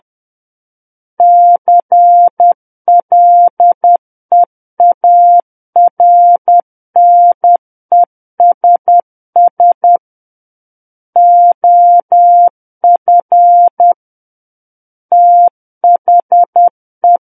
Morse code, heard as a keyed tone, about 10 wpm, 700 Hz. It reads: CLEARNESS OF THE
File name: Book One / 01-Chapter 1 - 10 WPM